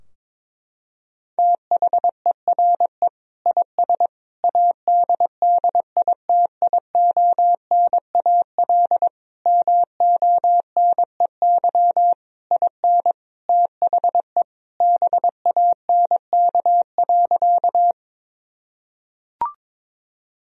Morse code, heard as a keyed tone, about 22 wpm, 700 Hz. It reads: THERE IS ADDITIONAL MONEY IN THE BANK.
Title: Sentences from Top 1000 Words - Review Part 4 - 22wpm